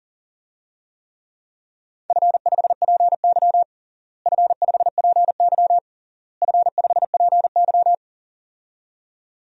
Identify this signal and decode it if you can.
Morse code, heard as a keyed tone, about 40 wpm, 700 Hz. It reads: F5PY F5PY F5PY
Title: DXCC-100 Call Signs - 4 Characters - Repeated 3x - 40wpm